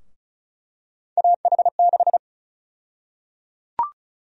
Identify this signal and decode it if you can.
Morse code, heard as a keyed tone, about 34 wpm, 700 Hz. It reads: AH6
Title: USA Call Sign Prefixes - Repeated 2x - 35wpm